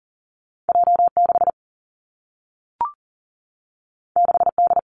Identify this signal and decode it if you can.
Morse code, heard as a keyed tone, about 42 wpm, 700 Hz. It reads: J 6 6 D
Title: Two Challenging Letter-Number Combinations 40wpm